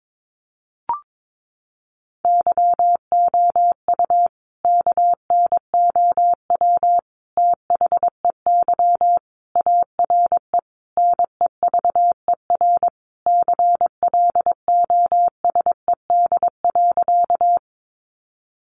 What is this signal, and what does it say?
Morse code, heard as a keyed tone, about 22 wpm, 700 Hz. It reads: YOU KNOW THEY ARE NEVER CLOSED.
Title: Sentences from Top 1400 Words 22wpm